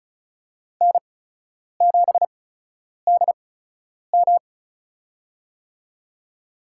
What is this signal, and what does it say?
Morse code, heard as a keyed tone, about 36 wpm, spaced as for 10 wpm, 700 Hz. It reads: N7DM
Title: USA Call Signs - 4 Characters - 8x Character Spacing - 35wpm